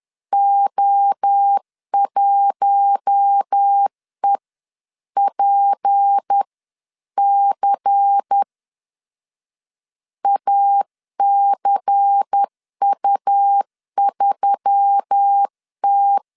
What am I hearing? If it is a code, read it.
Morse code, heard as a keyed tone, about 10 wpm, spaced as for 9 wpm, 800 Hz. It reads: O1EPC ACU3T
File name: CW / DET / 1 6 1500C S C 10wpm 4252